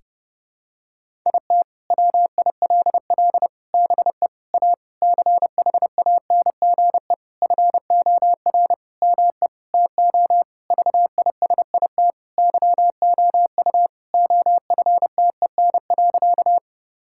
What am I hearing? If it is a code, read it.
Morse code, heard as a keyed tone, about 30 wpm, 700 Hz. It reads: IT WILL BE A CHANGE FOR ME TO VISIT YOU OFTEN.
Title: Sentences from Top 1000 Words - Review Part 6 - 30wpm